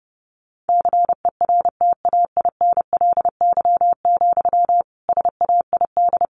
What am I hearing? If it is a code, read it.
Morse code, heard as a keyed tone, about 30 wpm, 700 Hz. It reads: CERTAINLY, SAID
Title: Sentences from Top 1500 Words 30wpm